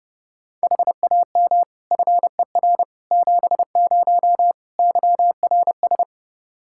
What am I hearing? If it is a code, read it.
Morse code, heard as a keyed tone, about 32 wpm, 700 Hz. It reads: HAM FER 70 YRS
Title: Practice QSO - Exchange 3 - More Info - 30wpm